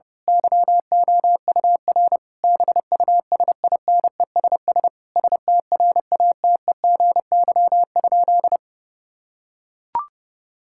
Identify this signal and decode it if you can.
Morse code, heard as a keyed tone, about 30 wpm, 700 Hz. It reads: YOUR BUSINESS STRATEGY?
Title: Sentences from Top 1000 Words - Review Part 6 - 30wpm